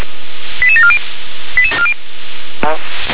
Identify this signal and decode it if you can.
Five-tone selective call (ZVEI), heard as quick runs of five tones, tones of about 70 ms, with speech. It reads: 88740